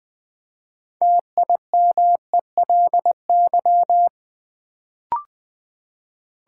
Morse code, heard as a keyed tone, about 20 wpm, 700 Hz. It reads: TIMELY
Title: Build A Word - 3 Letters At A Time - 6-Letter Words - 20wpm